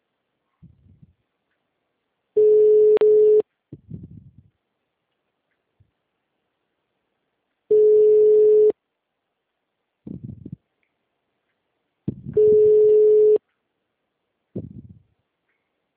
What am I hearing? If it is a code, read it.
Ringback tone (Europe, 425 Hz).